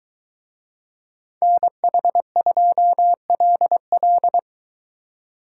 Morse code, heard as a keyed tone, about 23 wpm, 700 Hz. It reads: NH2LL